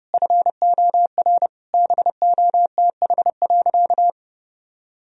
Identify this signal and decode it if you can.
Morse code, heard as a keyed tone, about 30 wpm, 700 Hz. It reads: FOR BOTH.
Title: Sentences from Top 1000 Words - Review Part 3 - 30wpm